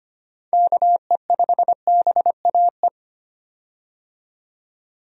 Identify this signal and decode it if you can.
Morse code, heard as a keyed tone, about 25 wpm, 700 Hz. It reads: KE5BAE
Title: USA Call Signs - 6 Characters - 25wpm